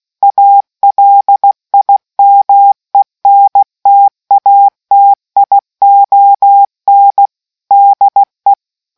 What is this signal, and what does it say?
Morse code, heard as a keyed tone, about 16 wpm, 800 Hz. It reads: ALIMENTATION DE